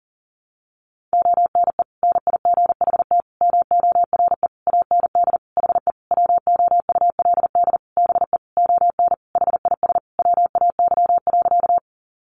Morse code, heard as a keyed tone, about 40 wpm, 700 Hz. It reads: ONE NIGHT MORE AND HE WOULD BE ON HIS WAY.
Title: Sentences from Top 300 Words 40wpm